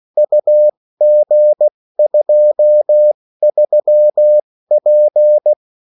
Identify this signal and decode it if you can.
Morse code, heard as a keyed tone, about 16 wpm, spaced as for 14 wpm, 600 Hz. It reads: UG23P
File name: morse mp3 / revision9-vitesse 14 mots